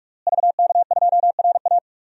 Morse code, heard as a keyed tone, about 46 wpm, 700 Hz. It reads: VK2RA
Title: DXCC-100 Call Signs - 5 Characters - Repeated 3x - 45wpm